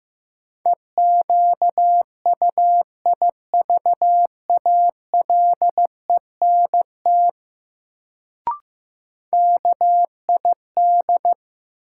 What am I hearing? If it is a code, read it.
Morse code, heard as a keyed tone, about 15 wpm, 700 Hz. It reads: EQUIVALENT KID
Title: Sentences from Top 1600 Words - New Word Drill - 15wpm